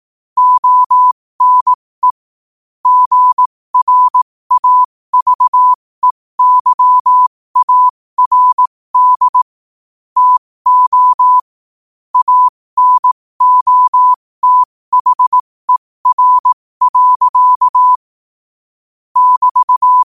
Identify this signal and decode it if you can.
Morse code, heard as a keyed tone, about 18 wpm, spaced as for 15 wpm, 1000 Hz. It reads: ONE GRAVEYARD TO ANOTHER. <BT>
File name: qotdfiles / 15wpm